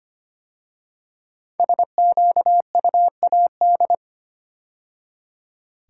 Morse code, heard as a keyed tone, about 25 wpm, 700 Hz. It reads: SQUAD
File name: 5 Letter Words - First 3 Letters Given - 25wpm